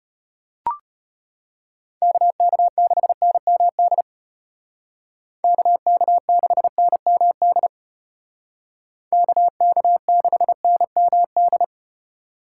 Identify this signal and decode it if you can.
Morse code, heard as a keyed tone, about 34 wpm, 700 Hz. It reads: KK6NMD KK6NMD KK6NMD